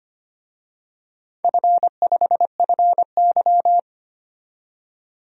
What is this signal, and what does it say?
Morse code, heard as a keyed tone, about 25 wpm, 700 Hz. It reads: F5FY
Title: DXCC-100 Call Signs - 4 Characters - 25wpm